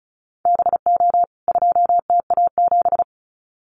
Morse code, heard as a keyed tone, about 34 wpm, 700 Hz. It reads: BO 2TA7
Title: CWT Exchange 35wpm